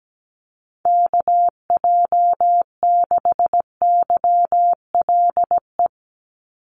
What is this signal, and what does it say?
Morse code, heard as a keyed tone, about 17 wpm, 700 Hz. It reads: KJ6YLE
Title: USA Call Signs 17wpm